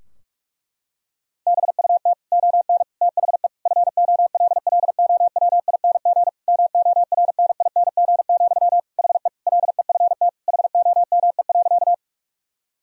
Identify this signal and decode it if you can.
Morse code, heard as a keyed tone, about 41 wpm, 700 Hz. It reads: BUT ON THE FOLLOWING MORNING, HE LEFT HOME.
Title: Sentences from Top 1000 Words - Review Part 4 - 45wpm